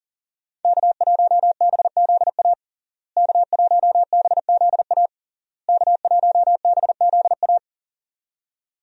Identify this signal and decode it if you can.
Morse code, heard as a keyed tone, about 40 wpm, 700 Hz. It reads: K1BZA K1BZA K1BZA
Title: USA Call Signs - 5 Characters - Repeated 3x - 40wpm